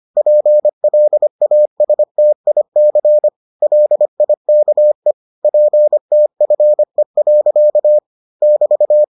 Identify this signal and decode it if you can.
Morse code, heard as a keyed tone, about 25 wpm, 600 Hz. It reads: PLASTIC LIKE PTFE. <BT>